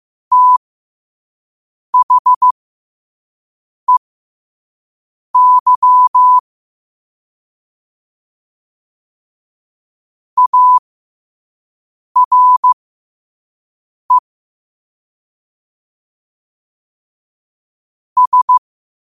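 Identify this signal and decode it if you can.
Morse code, heard as a keyed tone, about 15 wpm, spaced as for 5 wpm, 1000 Hz. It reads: THEY ARE S
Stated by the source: Morse code sent at 5 wpm for learning.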